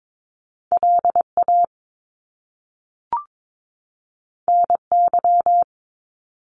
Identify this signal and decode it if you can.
Morse code, heard as a keyed tone, about 22 wpm, 700 Hz. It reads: LA NY
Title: US States 22wpm